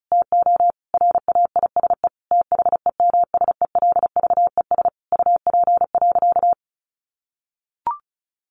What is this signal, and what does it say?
Morse code, heard as a keyed tone, about 34 wpm, 700 Hz. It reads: TO RAISE THEMSELVES UP.
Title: Sentences from Top 1400 Words 35wpm